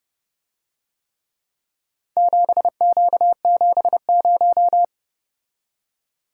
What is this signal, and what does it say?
Morse code, heard as a keyed tone, about 30 wpm, 700 Hz. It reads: 7Q70